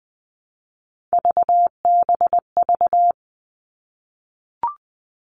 Morse code, heard as a keyed tone, about 20 wpm, 700 Hz. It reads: VBV
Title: All Cut Numbers - 3-Digit 20wpm